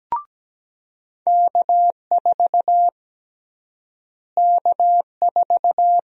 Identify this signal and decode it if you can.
Morse code, heard as a keyed tone, about 17 wpm, 700 Hz. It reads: K4 K4
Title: USA Call Sign Prefixes - Repeated 2x - 17wpm